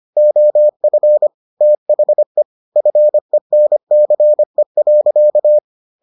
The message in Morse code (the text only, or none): OF THE FENCE.